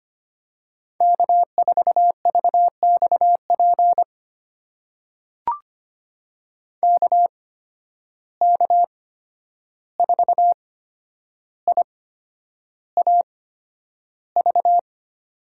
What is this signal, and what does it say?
Morse code, heard as a keyed tone, about 25 wpm, spaced as for 10 wpm, 700 Hz. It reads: K4VXP KK4IAV
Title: USA Call Signs - 8x Character Spacing - 25wpm